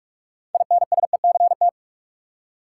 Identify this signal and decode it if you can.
Morse code, heard as a keyed tone, about 46 wpm, 700 Hz. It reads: INSECT
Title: Six Letter Words - 45wpm